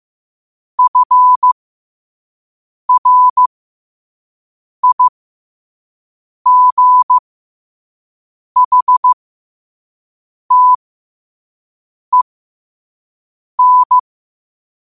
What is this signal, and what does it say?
Morse code, heard as a keyed tone, about 15 wpm, spaced as for 5 wpm, 1000 Hz. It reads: FRIGHTEN
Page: New quotes every day in morse code at 5 Words per minute.